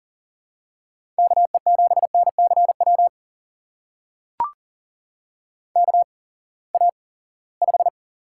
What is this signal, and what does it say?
Morse code, heard as a keyed tone, about 40 wpm, spaced as for 23 wpm, 700 Hz. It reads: KE7NCW KA5
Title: USA Call Signs - 6 Characters - 8x Character Spacing - 40wpm